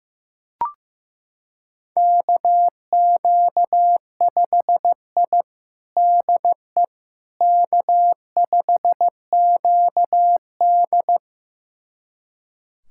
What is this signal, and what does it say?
Morse code, heard as a keyed tone, about 16 wpm, 700 Hz. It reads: KQ5I DE K5QD
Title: USA Callsign DE Callsign - 4 Characters - 15wpm